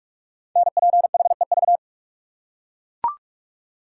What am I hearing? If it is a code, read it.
Morse code, heard as a keyed tone, about 44 wpm, 700 Hz. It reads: NPHEV
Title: Code Groups - 5 Letters - 45wpm